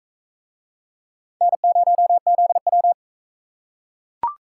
Morse code, heard as a keyed tone, about 37 wpm, 700 Hz. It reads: N0ZW